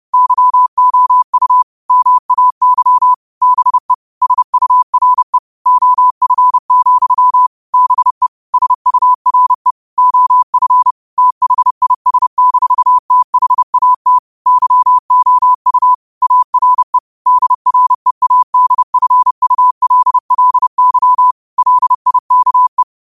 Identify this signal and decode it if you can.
Morse code, heard as a keyed tone, about 30 wpm, 1000 Hz. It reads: YOU MAY BE SURE OF, BE SURE OF THIS-THAT YOU ARE DREADFULLY LIKE